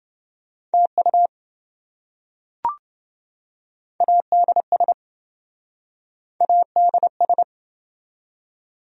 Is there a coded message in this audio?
Yes: Morse code from 0.7 to 7.4 s, heard as a keyed tone, about 27 wpm, 700 Hz. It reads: TU ADS ADS